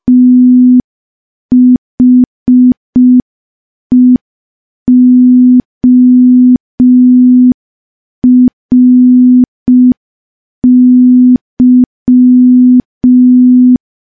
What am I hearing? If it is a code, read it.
Morse code, heard as a keyed tone, about 5 wpm, 250 Hz. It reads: THEORY
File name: morse / morse-training-05WPM